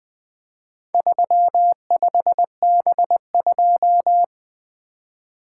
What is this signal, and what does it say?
Morse code, heard as a keyed tone, about 22 wpm, 700 Hz. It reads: 35B2
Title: Speed-Racing - Four Letter-Number Combinations - 20wpm